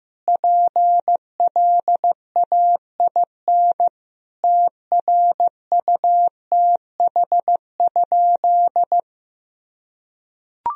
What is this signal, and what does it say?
Morse code, heard as a keyed tone, about 14 wpm, 700 Hz. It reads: PLAIN TRUTH?
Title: Sentences from Top 1000 Words - Review Part 2 - 15wpm